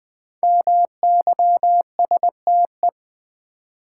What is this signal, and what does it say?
Morse code, heard as a keyed tone, about 20 wpm, 700 Hz. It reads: MYSTE
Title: Build A Word - 5 Letters At A Time - 10-Letter Words - 20wpm